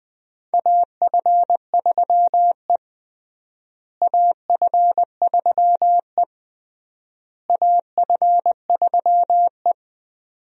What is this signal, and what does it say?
Morse code, heard as a keyed tone, about 20 wpm, 700 Hz. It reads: AF3E AF3E AF3E
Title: USA Call Signs - 4 Characters - Repeated 3x - 20wpm